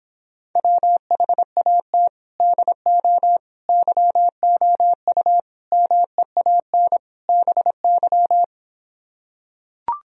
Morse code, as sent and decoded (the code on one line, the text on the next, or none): .-- .... .- - / -.. --- / -.-- --- ..- / -- . .- -. / -... -.--
WHAT DO YOU MEAN BY